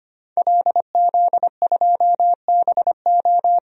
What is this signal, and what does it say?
Morse code, heard as a keyed tone, about 25 wpm, 700 Hz. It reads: LZ2BO
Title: DXCC-100 Call Signs - 5 Characters - 25wpm